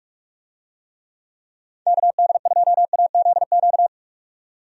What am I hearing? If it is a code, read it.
Morse code, heard as a keyed tone, about 46 wpm, 700 Hz. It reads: KD2AZQ
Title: USA Call Signs - 6 Characters - 45wpm